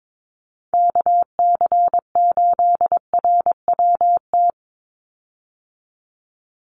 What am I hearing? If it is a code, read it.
Morse code, heard as a keyed tone, about 22 wpm, 700 Hz. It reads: KC8RWT
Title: USA Call Signs 22wpm